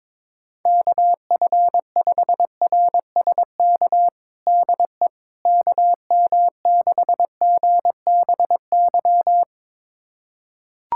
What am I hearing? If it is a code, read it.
Morse code, heard as a keyed tone, about 22 wpm, 700 Hz. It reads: KF5RSK DE KM6GBY
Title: USA Callsign DE Callsign - 6 Characters - 22wpm